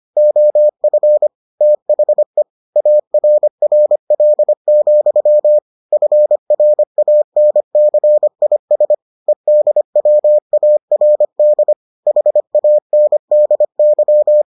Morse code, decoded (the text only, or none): OF THE ARRL, FRANCIS EDWARD HANDY